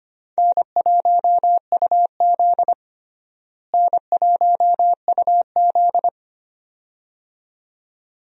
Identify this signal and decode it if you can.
Morse code, heard as a keyed tone, about 25 wpm, 700 Hz. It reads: N1UZ N1UZ
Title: USA Call Signs - 4 Characters - Repeated 2x - 25wpm